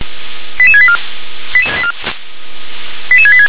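Five-tone selective call (ZVEI), heard as quick runs of five tones, tones of about 70 ms. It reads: 88684 88686
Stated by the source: ZVEI five-tone selective call received on SDR.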